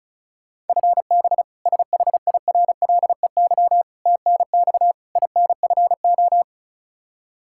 Morse code, heard as a keyed tone, about 36 wpm, 700 Hz. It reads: FB SHIRLEY TNX INFO
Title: K1USN SST QSO Practice 35wpm